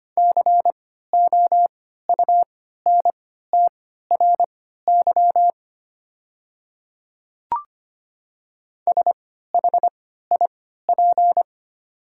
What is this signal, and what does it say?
Morse code, heard as a keyed tone, about 25 wpm, spaced as for 14 wpm, 700 Hz. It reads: COUNTRY SHIP
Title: 3x Character Spacing - Top 500 Words - 25wpm